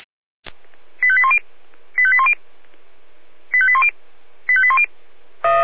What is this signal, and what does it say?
Five-tone selective call (ZVEI), heard as quick runs of five tones, tones of about 70 ms. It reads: 85810 85810 85810 85810